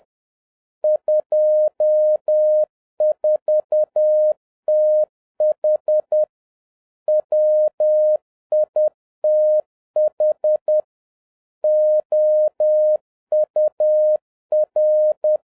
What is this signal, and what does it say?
Morse code, heard as a keyed tone, about 10 wpm, 610 Hz. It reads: 24TH WITH OUR